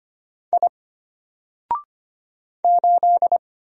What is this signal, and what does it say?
Morse code, heard as a keyed tone, about 25 wpm, 700 Hz. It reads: I 8
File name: Single-Letter-Number-25wpm